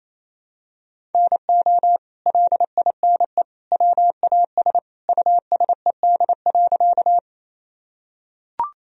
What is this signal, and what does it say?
Morse code, heard as a keyed tone, about 28 wpm, 700 Hz. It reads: NO LINE WAS USED.
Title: Sentences from Top 1000 Words - Review Part 1 - 28wpm